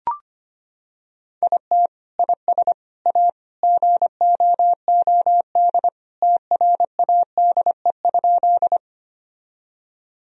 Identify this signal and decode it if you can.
Morse code, heard as a keyed tone, about 25 wpm, 700 Hz. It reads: IT IS A GOOD TRADE?
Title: Sentences from Top 1000 Words - Review Part 4 - 25wpm